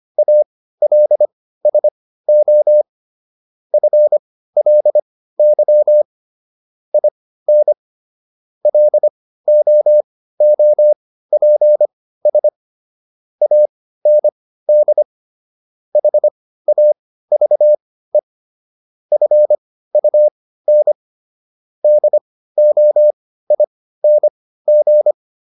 Morse code, heard as a keyed tone, about 25 wpm, spaced as for 15 wpm, 600 Hz. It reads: ALSO FLY IN LOOPS AND HAVE FUN DOING